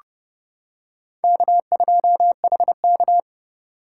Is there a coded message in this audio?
Yes: Morse code from 1.2 to 3.2 s, heard as a keyed tone, about 28 wpm, 700 Hz. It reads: K2HK